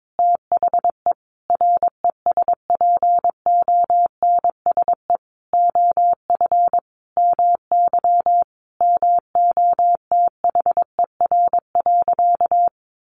Morse code, heard as a keyed tone, about 22 wpm, 700 Hz. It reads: THE RESPONSE OF MY MOTHER.
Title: Sentences from Top 1400 Words 22wpm